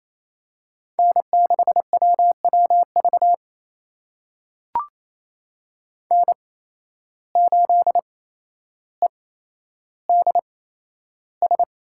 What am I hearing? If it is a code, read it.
Morse code, heard as a keyed tone, about 28 wpm, spaced as for 12 wpm, 700 Hz. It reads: N6WWV N8EDS
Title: USA Call Signs - 8x Character Spacing - 28wpm